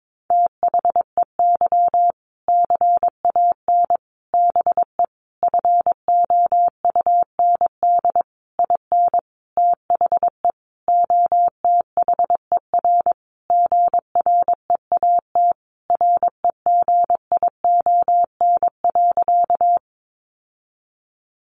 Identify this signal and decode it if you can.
Morse code, heard as a keyed tone, about 22 wpm, 700 Hz. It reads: THEY CAN BE FOUND IN THE OTHER GREAT REGION.
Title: Sentences from Top 1500 Words 22wpm